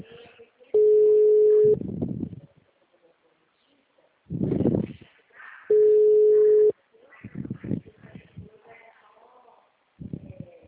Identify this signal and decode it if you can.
Ringback tone (Europe, 425 Hz).
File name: out- -2048- - - .7437